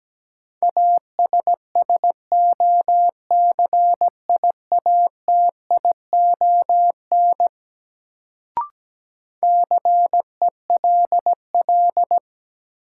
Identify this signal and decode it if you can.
Morse code, heard as a keyed tone, about 17 wpm, 700 Hz. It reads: ASSOCIATION CELL
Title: Sentences from Top 1100 Words - New Word Drill Alt - 17wpm